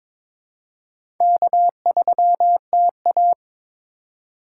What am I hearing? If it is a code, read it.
Morse code, heard as a keyed tone, about 21 wpm, 690 Hz. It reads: K3TA
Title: USA Call Signs - 4 Characters - 3x Character Spacing - 22wpm